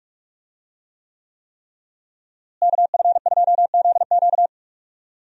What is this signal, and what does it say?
Morse code, heard as a keyed tone, about 45 wpm, 700 Hz. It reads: KF2ZQ
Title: USA Call Signs - 5 Characters - Repeated 3x - 45wpm